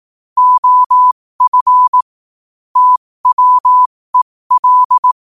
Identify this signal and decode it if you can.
Morse code, heard as a keyed tone, about 18 wpm, spaced as for 15 wpm, 1000 Hz. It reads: OF TWEL
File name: qotdfiles / 15wpm